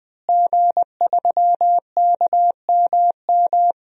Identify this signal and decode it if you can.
Morse code, heard as a keyed tone, about 20 wpm, 700 Hz. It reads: G3KMM
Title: DXCC-100 Call Signs - 5 Characters - Repeated 3x - 20wpm